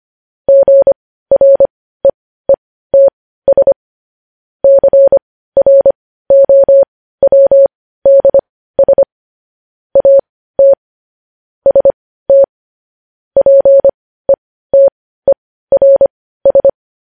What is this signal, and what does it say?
Morse code, heard as a keyed tone, about 25 wpm, spaced as for 15 wpm, 550 Hz. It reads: GREETS CROWDS AT ST PETERS
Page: news-headlines-in-morse-code-at-15-wpm